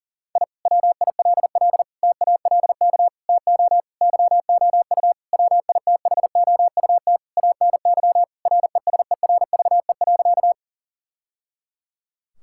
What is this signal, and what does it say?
Morse code, heard as a keyed tone, about 40 wpm, 700 Hz. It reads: I WILL TALK TO YOU WITHOUT ANY RESERVE.
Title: Sentences from Top 1000 Words - Review Part 6 - 40wpm